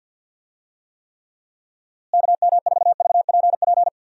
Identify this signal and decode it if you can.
Morse code, heard as a keyed tone, about 50 wpm, 700 Hz. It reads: KM4VPP
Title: USA Call Signs - 6 Characters - Repeated 3x - 50wpm